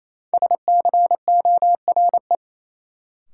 Morse code, heard as a keyed tone, about 28 wpm, 700 Hz. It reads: SCORE
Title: Sentences from Top 1600 Words - New Word Drill Alt - 28wpm